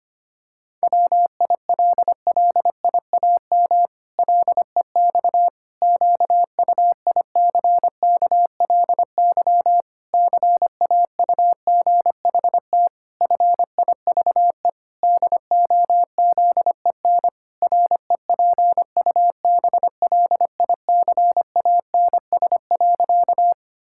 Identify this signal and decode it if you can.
Morse code, heard as a keyed tone, about 25 wpm, 700 Hz. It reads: WILLIAM LEX QUICKLY CAUGHT FIVE DOZEN REPUBLICANS.
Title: Pangram Challenge v2 - 25wpm